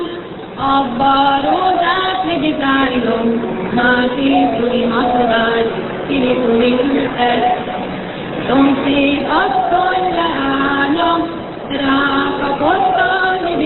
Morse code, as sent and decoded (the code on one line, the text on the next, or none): none